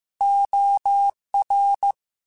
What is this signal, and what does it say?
Morse code, heard as a keyed tone, about 15 wpm, 800 Hz. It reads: OR
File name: Chuckles / Chuckles-mp3s / Glass Half Full @13